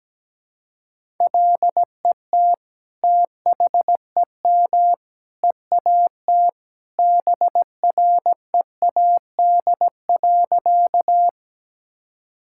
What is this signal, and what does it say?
Morse code, heard as a keyed tone, about 17 wpm, 700 Hz. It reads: LET THEM EAT BREAD.
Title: Sentences from Top 1600 Words 17wpm